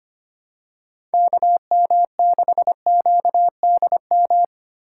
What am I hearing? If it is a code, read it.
Morse code, heard as a keyed tone, about 25 wpm, 700 Hz. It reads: KM6QDM
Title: USA Call Signs - 2x Character Spacing - 25wpm